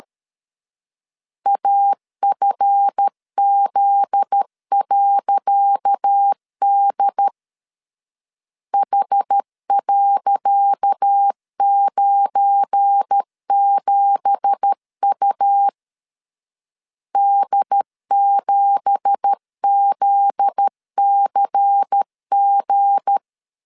Morse code, heard as a keyed tone, about 12 wpm, 800 Hz. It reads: AFZ.D H.97U D7ZCG
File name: LEZIONI / 10 SIGNS / 10C DOT 12wpm 2451